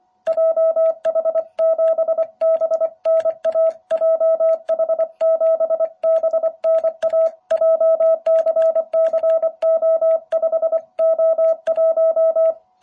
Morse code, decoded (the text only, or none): JH7BNA JH7BNA JCCO5O1